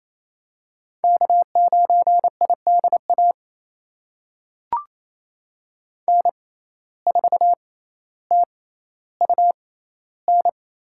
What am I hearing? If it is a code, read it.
Morse code, heard as a keyed tone, about 27 wpm, spaced as for 14 wpm, 700 Hz. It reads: K9IDA N4TUN
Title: USA Call Signs - 6x Character Spacing - 28wpm